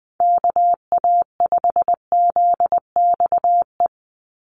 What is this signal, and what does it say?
Morse code, heard as a keyed tone, about 20 wpm, 700 Hz. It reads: KA5ZXE
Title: USA Call Signs 20wpm